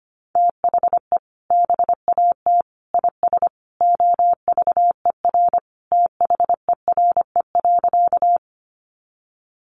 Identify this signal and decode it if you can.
Morse code, heard as a keyed tone, about 25 wpm, 700 Hz. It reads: THE BAT IS OVER THERE.
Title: Sentences from Top 1500 Words 25wpm